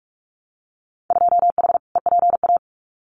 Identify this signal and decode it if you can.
Morse code, heard as a keyed tone, about 46 wpm, 700 Hz. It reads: 2H EPA
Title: Winter Field Day Exchanges 45wpm